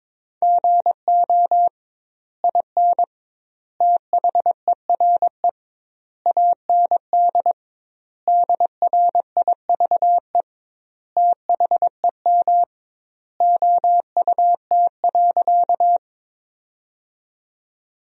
Morse code, heard as a keyed tone, about 22 wpm, 700 Hz. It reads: GO IN THERE AND DRIVE THEM OUT.
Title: 2x Word-Spacing - Sentences from Top 500 Words - 22wpm